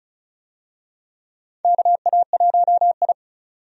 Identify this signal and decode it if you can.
Morse code, heard as a keyed tone, about 34 wpm, 700 Hz. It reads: KA1I
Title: USA Call Signs - 4 Characters - 35wpm